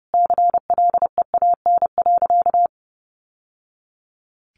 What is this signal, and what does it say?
Morse code, heard as a keyed tone, about 30 wpm, 700 Hz. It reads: CLEAN.
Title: Sentences from Top 1100 Words 30wpm